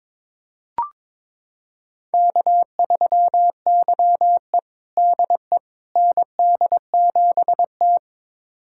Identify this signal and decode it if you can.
Morse code, heard as a keyed tone, about 22 wpm, 700 Hz. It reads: K3YE DE ND7T
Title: USA Callsign DE Callsign - 4 Characters - 22wpm